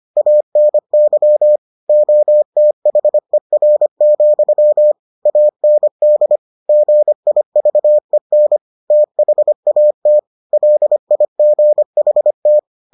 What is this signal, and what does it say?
Morse code, heard as a keyed tone, about 25 wpm, 600 Hz. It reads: ANY OTHER, AND GIVEN THAT LIGHT